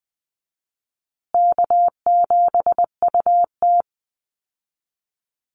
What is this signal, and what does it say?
Morse code, heard as a keyed tone, about 19 wpm, 700 Hz. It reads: K7UT
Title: Easy Call Signs Repeated 2x - 20wpm